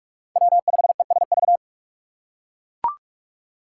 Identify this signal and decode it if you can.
Morse code, heard as a keyed tone, about 45 wpm, 700 Hz. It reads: W5ESV